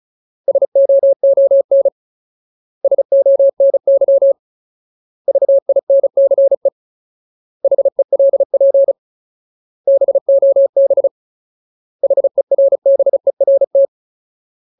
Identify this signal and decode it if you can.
Morse code, heard as a keyed tone, about 34 wpm, 550 Hz. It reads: SOON SONY VINCE HELP BOB HERBERT
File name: Ultimate Word List / 35WPM Ultimate